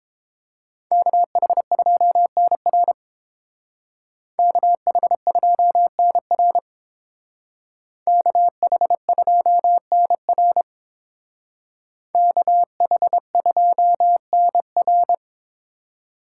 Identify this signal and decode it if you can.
Morse code, heard as a keyed tone, about 29 wpm, 700 Hz. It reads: KH2NR KH2NR KH2NR KH2NR